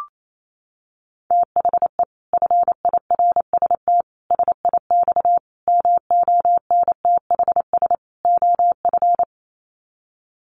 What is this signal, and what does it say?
Morse code, heard as a keyed tone, about 28 wpm, 700 Hz. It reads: THE FIRST SIX MONTHS OF
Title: Sets of 5 Words Encore2 28wpm